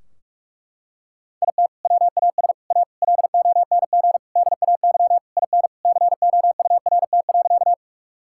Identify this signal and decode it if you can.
Morse code, heard as a keyed tone, about 41 wpm, 700 Hz. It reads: IT WAS A LONG DAY IN COURT.
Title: Sentences from Top 1600 Words 45wpm